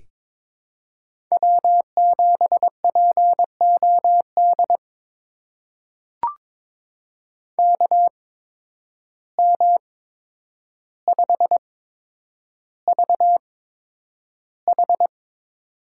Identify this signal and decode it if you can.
Morse code, heard as a keyed tone, about 22 wpm, spaced as for 9 wpm, 700 Hz. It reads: W7POD KM5VH
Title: USA Call Signs - 5 Characters - 8x Character Spacing - 22wpm